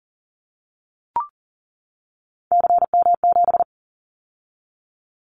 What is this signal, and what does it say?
Morse code, heard as a keyed tone, about 38 wpm, 690 Hz. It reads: CM7
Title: Callsign Prefixes for Top 100 DXCC - 40wpm